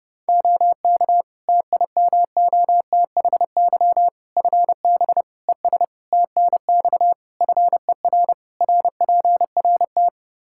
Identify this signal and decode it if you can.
Morse code, heard as a keyed tone, about 30 wpm, 700 Hz. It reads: OK TIMOTHY FB ES TNX FER RPRT
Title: Practice QSO - Exchange 2 - Info - 30wpm